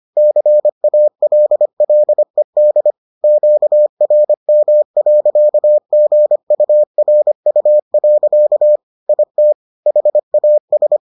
Morse code, heard as a keyed tone, about 25 wpm, 600 Hz. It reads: CALLED QRM.GURU. IT HAS